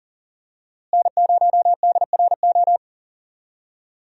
Morse code, heard as a keyed tone, about 39 wpm, 700 Hz. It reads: N0DRO
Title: USA Call Signs - 5 Characters - Repeated 3x - 40wpm